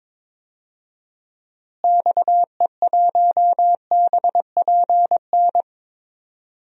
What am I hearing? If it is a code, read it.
Morse code, heard as a keyed tone, about 22 wpm, 700 Hz. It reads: XE1BPN
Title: DXCC-100 Call Signs - 6 Characters - Repeated 2x - 22wpm